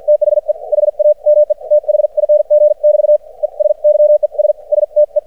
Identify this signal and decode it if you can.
Morse code, heard as a keyed tone, about 56 wpm, spaced as for 46 wpm, 600 Hz. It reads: THE HAME THING <BT> ES YES STE
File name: archived / CRSnet